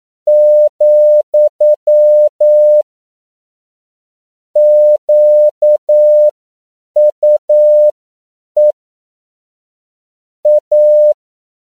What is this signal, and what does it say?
Morse code, heard as a keyed tone, about 9 wpm, spaced as for 7 wpm, 600 Hz. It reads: , QUE A
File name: CW / 10PPM